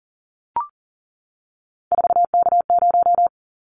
Morse code, heard as a keyed tone, about 40 wpm, 700 Hz. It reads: 4K0
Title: Callsign Prefixes for Top 100 DXCC - 40wpm